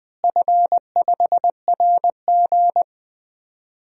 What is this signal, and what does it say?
Morse code, heard as a keyed tone, about 20 wpm, 700 Hz. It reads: F5RG
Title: DXCC-100 Call Signs - 4 Characters - Repeated 3x - 20wpm